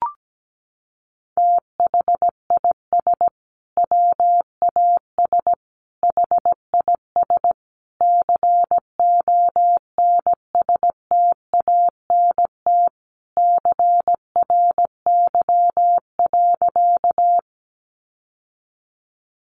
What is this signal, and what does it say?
Morse code, heard as a keyed tone, about 17 wpm, 700 Hz. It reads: THIS WAS HIS CONSTANT CRY.
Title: Sentences from Top 1500 Words 17wpm